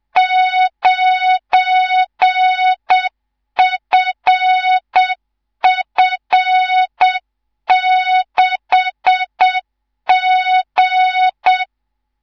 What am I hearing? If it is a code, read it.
Morse code, heard as a keyed tone, about 7 wpm, 3700 Hz, with noise. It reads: 9FF6G
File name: Morze-HA2MN / hang / 19 GYAKORLAT